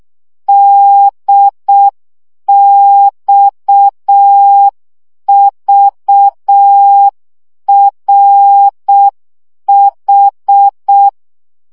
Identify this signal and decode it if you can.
Morse code, heard as a keyed tone, about 6 wpm, 800 Hz. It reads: DXVRH
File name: morze vizsga 2011-3 5